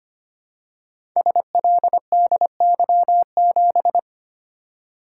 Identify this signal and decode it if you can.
Morse code, heard as a keyed tone, about 25 wpm, 700 Hz. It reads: SLDY7
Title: Code Groups - 5 Letter:Numbers - 25wpm